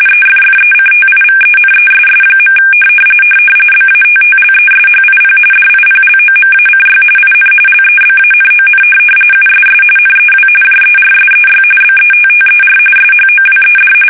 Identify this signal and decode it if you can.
RTTY, about 45.45 baud. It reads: EFFECTIVENESS. BATTLESHIP DIPLOMACY IN LEBANON USS NEW JERSEY BETTER SHOWED
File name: fsk850 / battleshipnj